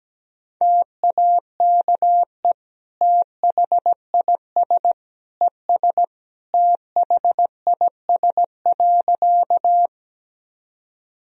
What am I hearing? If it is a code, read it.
Morse code, heard as a keyed tone, about 17 wpm, 700 Hz. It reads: TAKE THIS ES THIS.
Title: Sentences from Top 200 Words and Abbreviations 17wpm